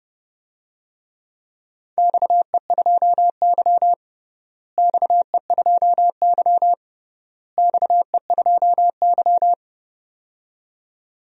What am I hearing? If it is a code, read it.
Morse code, heard as a keyed tone, about 30 wpm, 700 Hz. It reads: XE2Y XE2Y XE2Y
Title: DXCC-100 Call Signs - 4 Characters - Repeated 3x - 30wpm